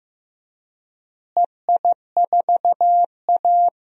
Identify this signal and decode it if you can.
Morse code, heard as a keyed tone, about 15 wpm, 700 Hz. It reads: EI4A
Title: DXCC-100 Call Signs - 4 Characters - Repeated 2x - 15wpm